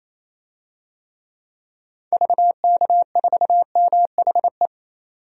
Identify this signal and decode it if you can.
Morse code, heard as a keyed tone, about 28 wpm, 700 Hz. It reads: VK4MHE
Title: DXCC-100 Call Signs - 6 Characters - 28wpm